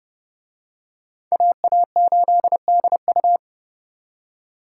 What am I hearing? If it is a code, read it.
Morse code, heard as a keyed tone, about 30 wpm, 700 Hz. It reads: AA8DU